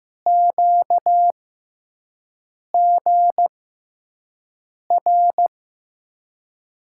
Morse code, heard as a keyed tone, about 15 wpm, spaced as for 5 wpm, 700 Hz. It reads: QGR
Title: USA Call Signs - 6x Character Spacing - 15wpm